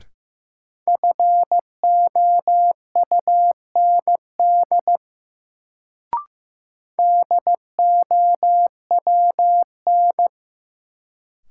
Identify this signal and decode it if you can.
Morse code, heard as a keyed tone, about 16 wpm, 700 Hz. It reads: FOUND DOWN
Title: Sentences from Top 300 Words - New Word Drill - 15wpm